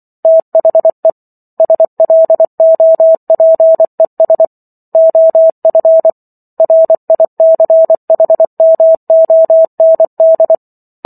Morse code, heard as a keyed tone, about 24 wpm, 650 Hz. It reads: THE SLOPES OF RICHMOND